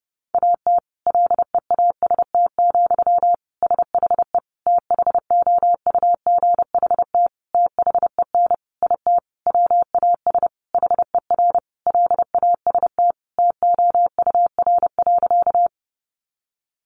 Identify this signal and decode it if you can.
Morse code, heard as a keyed tone, about 30 wpm, 700 Hz. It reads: AT LEAST, SHE THOUGHT THEN IT WAS HER LAST TOUR.
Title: Sentences from Top 1400 Words 30wpm